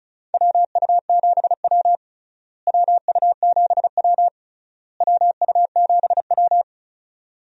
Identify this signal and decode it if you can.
Morse code, heard as a keyed tone, about 36 wpm, 700 Hz. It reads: WU7W WU7W WU7W
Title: USA Call Signs - 4 Characters - Repeated 3x - 35wpm